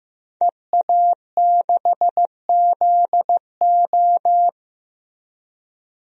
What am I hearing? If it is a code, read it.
Morse code, heard as a keyed tone, about 15 wpm, 700 Hz. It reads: EA6ZO
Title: DXCC-100 Call Signs - 5 Characters - Repeated 3x - 15wpm